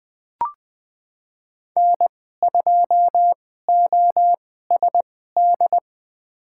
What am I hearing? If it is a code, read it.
Morse code, heard as a keyed tone, about 21 wpm, spaced as for 15 wpm, 700 Hz. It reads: N2OSD
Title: USA Call Signs - 2x Character Spacing - 20wpm